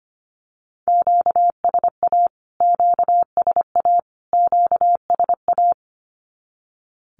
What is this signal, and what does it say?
Morse code, heard as a keyed tone, about 25 wpm, 700 Hz. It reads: QSA QSA QSA
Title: Q-Codes Repeated 3x 25wpm